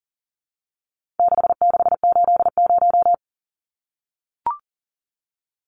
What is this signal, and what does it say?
Morse code, heard as a keyed tone, about 40 wpm, 700 Hz. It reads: 6680
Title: Numbers - 4-Digit 40wpm